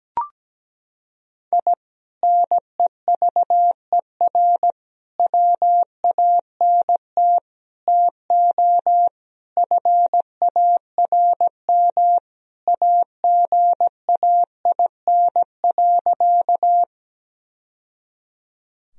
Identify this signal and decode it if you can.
Morse code, heard as a keyed tone, about 17 wpm, 700 Hz. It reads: I NEVER WANT TO FARM AGAIN.
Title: Sentences from Top 1000 Words - Review Part 2 - 17wpm